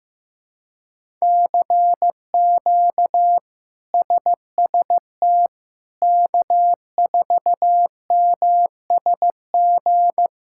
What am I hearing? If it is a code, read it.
Morse code, heard as a keyed tone, about 15 wpm, 700 Hz. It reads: CQ SST K4MSG
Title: K1USN SST QSO Practice - 15wpm